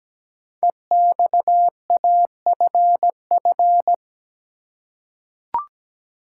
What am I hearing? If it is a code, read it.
Morse code, heard as a keyed tone, about 17 wpm, 700 Hz. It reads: EXAFF
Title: Code Groups - 5 Letters - 17wpm